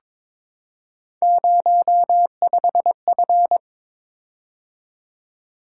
Morse code, heard as a keyed tone, about 22 wpm, 700 Hz. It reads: 05F
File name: practice / 22wpm / Three Letter-Number Combinations - 22wpm